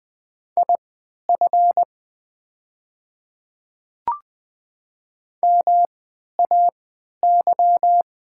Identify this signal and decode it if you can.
Morse code, heard as a keyed tone, about 20 wpm, spaced as for 11 wpm, 700 Hz. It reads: IF MAY
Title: 3x Character Spacing - Top 300 Words - 20wpm